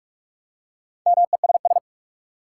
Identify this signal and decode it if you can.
Morse code, heard as a keyed tone, about 42 wpm, 700 Hz. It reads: MESS
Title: Four Letter Words - 45wpm